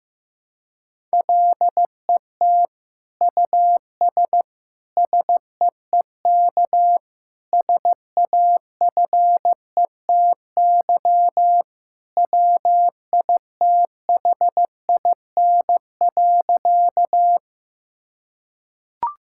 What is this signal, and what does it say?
Morse code, heard as a keyed tone, about 15 wpm, 700 Hz. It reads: LET US SEEK SAFETY WITHIN.
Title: Sentences from Top 1600 Words 15wpm